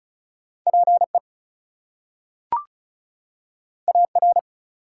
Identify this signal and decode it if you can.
Morse code, heard as a keyed tone, about 36 wpm, 700 Hz. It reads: PE AR
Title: US States and Canadian Provinces 35wpm